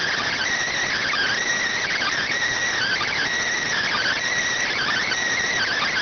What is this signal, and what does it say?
Emergency Alert System (SAME) tones (header bursts), heard as harsh buzzing bursts. It is heard from 0.9 to 5.2 s.